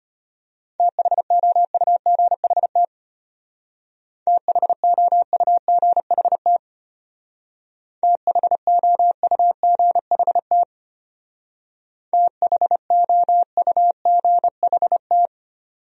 Morse code, heard as a keyed tone, about 31 wpm, 700 Hz. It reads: THOUGHT THOUGHT THOUGHT THOUGHT
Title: Speed-Racing Top 500 Words 25wpm